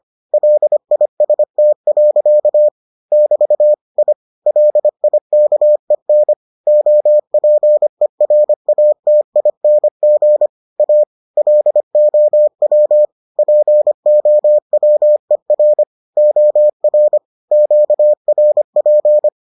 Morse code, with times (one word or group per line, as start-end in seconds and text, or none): 0.3-2.7 LIST.
3.1-3.7 <BT>
4.0-4.1 I
4.5-6.3 LIKEN
6.7-10.5 OPERATING
10.8-11.0 A
11.4-13.1 LOW
13.4-15.8 POWER
16.2-17.2 OR
17.5-19.3 QRP